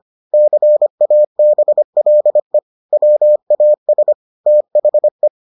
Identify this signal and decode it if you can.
Morse code, heard as a keyed tone, about 25 wpm, 600 Hz. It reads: CABLE WAS THE